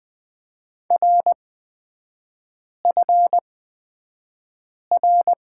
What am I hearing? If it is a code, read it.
Morse code, heard as a keyed tone, about 20 wpm, spaced as for 5 wpm, 700 Hz. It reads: RFR
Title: 236 Lesson 38e - Letter q - Three Characters 20wpm